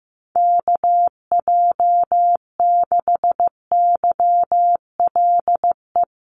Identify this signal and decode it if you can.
Morse code, heard as a keyed tone, about 15 wpm, 700 Hz. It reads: KJ6YLE
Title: USA Call Signs 15wpm